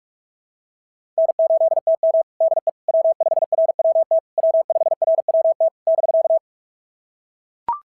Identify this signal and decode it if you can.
Morse code, heard as a keyed tone, about 44 wpm, 690 Hz. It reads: N8TM DE W5RWT W5RWT <BK>
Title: SOTA QSO Practice 45wpm